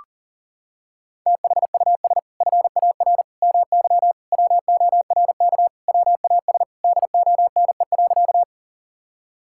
Morse code, heard as a keyed tone, about 40 wpm, 700 Hz. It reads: THUS FAR MY WORK WAS DONE.
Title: Sentences from Top 1000 Words - Review Part 5 - 40wpm